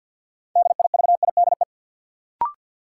Morse code, heard as a keyed tone, about 48 wpm, 700 Hz. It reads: DIVIDE
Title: Sentences from Top 1600 Words - New Word Drill Alt - 50wpm